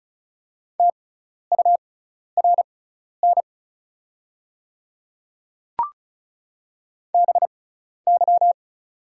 Morse code, heard as a keyed tone, about 36 wpm, spaced as for 12 wpm, 700 Hz. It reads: TURN BY
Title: 6x Character Spacing - Top 200 Words - 35wpm